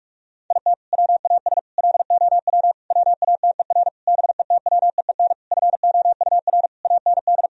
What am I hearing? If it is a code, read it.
Morse code, heard as a keyed tone, about 46 wpm, 700 Hz. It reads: IT WAS LOW WATER BETWEEN FOUR AND
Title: Sentences from Top 1000 Words - Review Part 2 - 45wpm